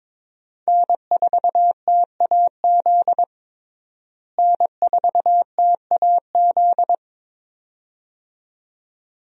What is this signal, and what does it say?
Morse code, heard as a keyed tone, about 22 wpm, 700 Hz. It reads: N4TAZ N4TAZ
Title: USA Call Signs - 5 Characters - Repeated 2x - 22wpm